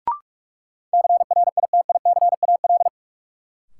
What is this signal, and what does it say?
Morse code, heard as a keyed tone, about 41 wpm, 700 Hz. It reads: CRITICAL
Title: Sentences from Top 1100 Words - New Word Drill Alt - 45wpm